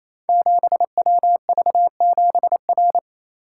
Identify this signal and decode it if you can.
Morse code, heard as a keyed tone, about 28 wpm, 700 Hz. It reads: 7WV7R
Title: Code Groups - 5 Letter:Numbers - 28wpm